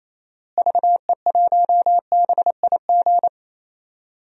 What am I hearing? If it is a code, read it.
Morse code, heard as a keyed tone, about 27 wpm, 700 Hz. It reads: VE1BIG
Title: DXCC-100 Call Signs - 6 Characters - Repeated 3x - 28wpm